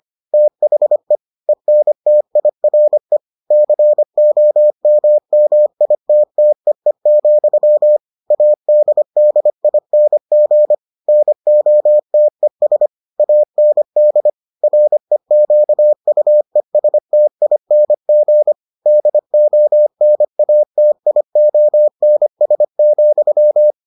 Morse code, heard as a keyed tone, about 25 wpm, 600 Hz. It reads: THE ENTIRE COMMITTEE, ADDING NOTES AND REQUESTING DONATIONS,